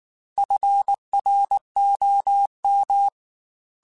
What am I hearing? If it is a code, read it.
Morse code, heard as a keyed tone, about 19 wpm, 800 Hz. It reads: FROM
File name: Weather Sets / Weather Sets-mp3s / Weather 4 17 @18